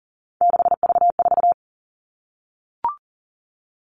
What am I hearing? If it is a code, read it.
Morse code, heard as a keyed tone, about 40 wpm, 700 Hz. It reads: 6V4
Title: All Cut Numbers - 3-Digit 40wpm